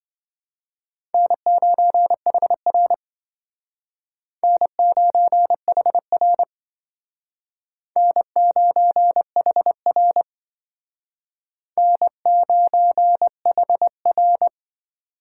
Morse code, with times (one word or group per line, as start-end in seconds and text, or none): 1.1-2.9 N9HR
4.4-6.4 N9HR
8.0-10.2 N9HR
11.8-14.5 N9HR